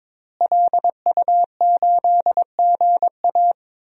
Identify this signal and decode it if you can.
Morse code, heard as a keyed tone, about 22 wpm, 700 Hz. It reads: LU8GA
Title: DXCC-100 Call Signs - 5 Characters - Repeated 2x - 22wpm